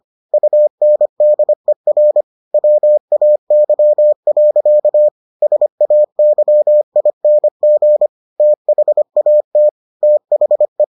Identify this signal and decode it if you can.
Morse code, heard as a keyed tone, about 25 wpm, 600 Hz. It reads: UNDER WAY. SAYING THAT THE